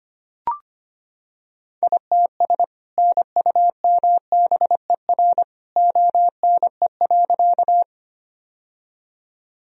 Morse code, heard as a keyed tone, about 25 wpm, 700 Hz. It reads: ITS NUMBER ONE.
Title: Sentences from Top 1000 Words - Review Part 1 - 25wpm